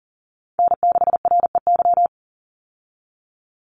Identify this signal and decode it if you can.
Morse code, heard as a keyed tone, about 40 wpm, 700 Hz. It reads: N6REY
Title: USA Call Signs 40wpm